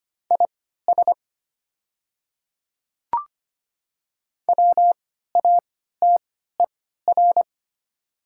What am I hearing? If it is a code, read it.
Morse code, heard as a keyed tone, about 25 wpm, spaced as for 14 wpm, 700 Hz. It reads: IS WATER
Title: 3x Character Spacing - Top 100 Words - 25wpm